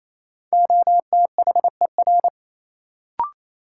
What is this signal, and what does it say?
Morse code, heard as a keyed tone, about 28 wpm, 700 Hz. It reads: OTHER
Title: Sentences from Top 100 Words - New Word Drill - 28wpm